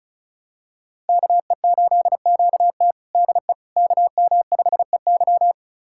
Morse code, heard as a keyed tone, about 34 wpm, 700 Hz. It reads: KE8QT DE KM5EY
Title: USA Callsign DE Callsign - 5 Characters - 35wpm